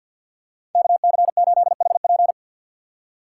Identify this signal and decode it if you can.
Morse code, heard as a keyed tone, about 53 wpm, 700 Hz. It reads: KC8HP
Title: USA Call Signs - 5 Characters - 4x Character Spacing - 50wpm